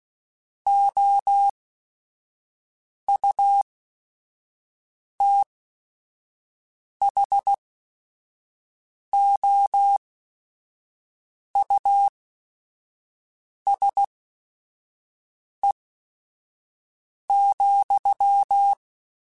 Morse code, heard as a keyed tone, about 16 wpm, spaced as for 5 wpm, 800 Hz. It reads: OUTHOUSE,
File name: General / General-mp3s / Houses @5